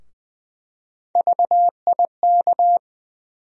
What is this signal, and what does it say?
Morse code, heard as a keyed tone, about 20 wpm, 700 Hz. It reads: VIK
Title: USA Call Sign Postfixes - 3 Letters 20wpm